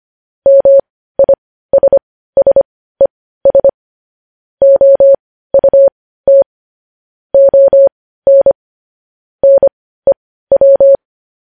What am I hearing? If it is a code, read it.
Morse code, heard as a keyed tone, about 25 wpm, spaced as for 15 wpm, 550 Hz. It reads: MISSES OUT ON NEW
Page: news headlines in morse code at 15 wpm